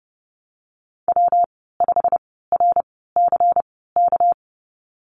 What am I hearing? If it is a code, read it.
Morse code, heard as a keyed tone, about 30 wpm, spaced as for 17 wpm, 700 Hz. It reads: W5RCK
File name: Lesson 37g - Letter x - Call Signs 30wpm